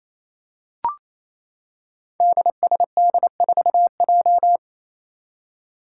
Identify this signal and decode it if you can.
Morse code, heard as a keyed tone, about 28 wpm, 700 Hz. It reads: DSD4J
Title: Five Challenging Letter-Number Combinations 28wpm